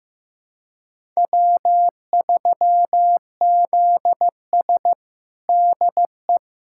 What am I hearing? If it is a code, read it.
Morse code, heard as a keyed tone, about 15 wpm, 700 Hz. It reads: W3ZS DE
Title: USA Callsign DE Callsign - 4 Characters - 15wpm